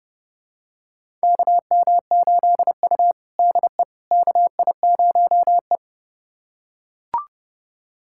Morse code, heard as a keyed tone, about 30 wpm, 700 Hz. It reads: KM8U DE KI0E
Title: USA Callsign DE Callsign - 4 Characters - 30wpm